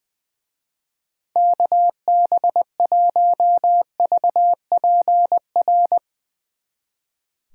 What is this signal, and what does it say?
Morse code, heard as a keyed tone, about 22 wpm, 700 Hz. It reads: KB1VPR
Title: Speed Racing - USA 2x3 Call Signs 20wpm